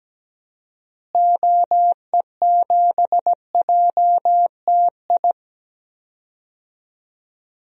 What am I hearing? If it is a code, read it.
Morse code, heard as a keyed tone, about 17 wpm, 700 Hz. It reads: OE7JTI
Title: DXCC-100 Call Signs - 6 Characters - Repeated 3x - 17wpm